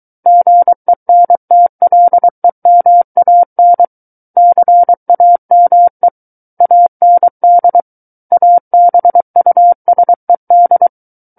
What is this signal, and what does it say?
Morse code, heard as a keyed tone, about 23 wpm, 700 Hz. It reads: GENTLEMAN CAME AND ABUSED
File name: Book One / 14-Chapter 14 - 23 WPM